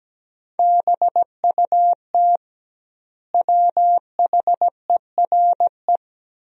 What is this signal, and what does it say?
Morse code, heard as a keyed tone, about 17 wpm, 700 Hz. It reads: BUT WHERE
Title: 2x Word-Spacing - Sentences from Top 200 Words - 17wpm